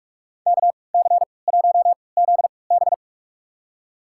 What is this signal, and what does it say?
Morse code, heard as a keyed tone, about 46 wpm, spaced as for 25 wpm, 700 Hz. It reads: KC1ZB
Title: USA Call Signs - 3x Character Spacing - 45wpm